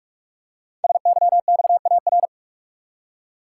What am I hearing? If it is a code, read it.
Morse code, heard as a keyed tone, about 44 wpm, 700 Hz. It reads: SYXAR
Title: Code Groups - 5 Letters - 45wpm